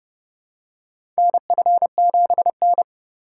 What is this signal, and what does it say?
Morse code, heard as a keyed tone, about 30 wpm, 700 Hz. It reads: NF7N
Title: USA Call Signs - 4 Characters - 4x Character Spacing - 30wpm